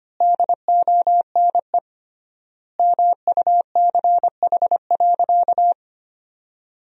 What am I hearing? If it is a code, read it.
Morse code, heard as a keyed tone, about 25 wpm, 700 Hz. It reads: DONE MUCH.
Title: 3x Word-Spacing - Sentences from Top 500 Words - 25wpm